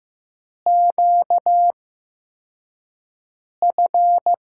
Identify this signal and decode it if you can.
Morse code, heard as a keyed tone, about 15 wpm, spaced as for 4 wpm, 700 Hz. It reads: QF
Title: USA Call Signs - 6 Characters - 8x Character Spacing - 15wpm